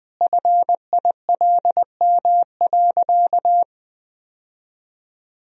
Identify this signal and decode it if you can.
Morse code, heard as a keyed tone, about 20 wpm, 700 Hz. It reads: FILM.
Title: Sentences from Top 1000 Words - Review Part 3 - 20wpm